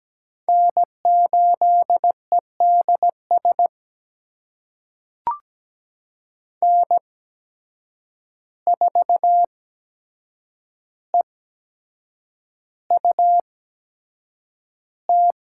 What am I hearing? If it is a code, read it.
Morse code, heard as a keyed tone, about 17 wpm, spaced as for 7 wpm, 700 Hz. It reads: N8EDS N4EUT
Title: USA Call Signs - 5 Characters - 8x Character Spacing - 17wpm